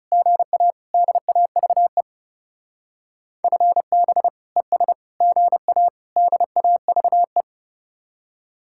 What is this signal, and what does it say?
Morse code, heard as a keyed tone, about 39 wpm, 700 Hz. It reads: GA DAVE FB ES GA DAVE